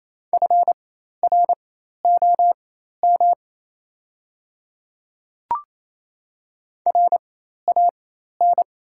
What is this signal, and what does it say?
Morse code, heard as a keyed tone, about 28 wpm, spaced as for 13 wpm, 700 Hz. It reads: FROM RAN
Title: 4x Character Spacing - Top 500 Words - 28wpm